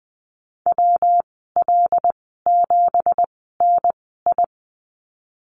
Morse code, heard as a keyed tone, about 20 wpm, spaced as for 15 wpm, 700 Hz. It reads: WL7NI
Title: 173 Lesson 28g - Number 9 - Call Signs 20wpm